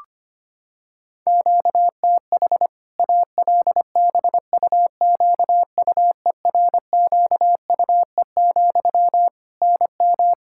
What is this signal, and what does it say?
Morse code, heard as a keyed tone, about 25 wpm, 700 Hz. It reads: QTH ALBUQUERQUE, NM
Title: Practice QSO - Exchange 1 - Beginning - 25wpm